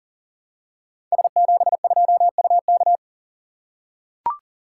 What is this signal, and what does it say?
Morse code, heard as a keyed tone, about 40 wpm, 700 Hz. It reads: S72UK
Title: Code Groups - 5 Letter:Numbers - 40wpm